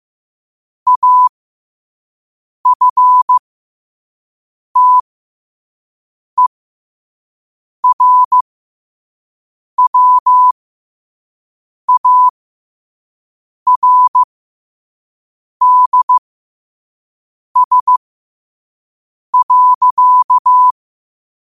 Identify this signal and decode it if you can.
Morse code, heard as a keyed tone, about 15 wpm, spaced as for 5 wpm, 1000 Hz. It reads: AFTERWARDS.
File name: qotdfiles / 5wpm